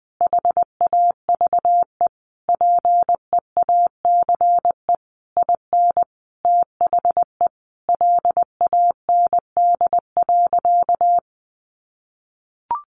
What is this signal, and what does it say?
Morse code, heard as a keyed tone, about 20 wpm, 700 Hz. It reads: HAVE PEACE IN THE LAND.
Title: Sentences from Top 1500 Words 20wpm